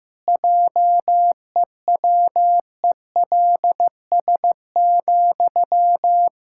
Morse code, heard as a keyed tone, about 15 wpm, 700 Hz. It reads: JEWELS,
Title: Pangram Challenge v2 - 15wpm